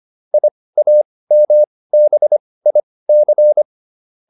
Morse code, heard as a keyed tone, about 25 wpm, spaced as for 18 wpm, 600 Hz. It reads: IAMBIC